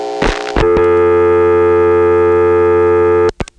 Dial tone (North America), 350 + 440 Hz, with noise.
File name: phonetrips / PrinceWmCoVA5HQ